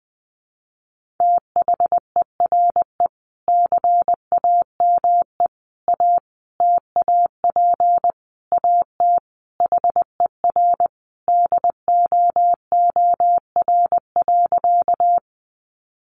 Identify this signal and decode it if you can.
Morse code, heard as a keyed tone, about 20 wpm, 700 Hz. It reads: THERE CAME A TAP AT HER DOOR.
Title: Sentences from Top 1400 Words 20wpm